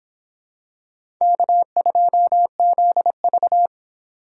Morse code, heard as a keyed tone, about 26 wpm, 700 Hz. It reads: K2ZV